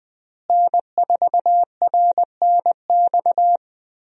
Morse code, heard as a keyed tone, about 20 wpm, 700 Hz. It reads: N4RNX
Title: USA Call Signs - 5 Characters - Repeated 2x - 20wpm